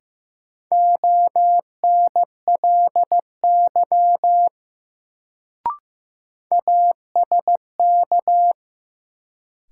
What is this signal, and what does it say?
Morse code, heard as a keyed tone, about 15 wpm, 700 Hz. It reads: ONLY ASK
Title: Sentences from Top 200 Words - New Word Drill - 15wpm